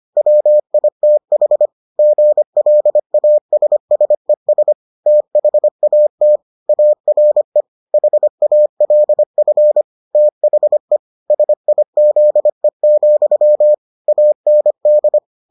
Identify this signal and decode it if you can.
Morse code, heard as a keyed tone, about 25 wpm, 600 Hz. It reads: WITH GLASSES THAT ARE HALF THE SIZE, AND